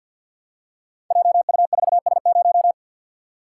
Morse code, heard as a keyed tone, about 46 wpm, 700 Hz. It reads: JU4S0
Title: Code Groups - 5 Letter:Numbers - 50wpm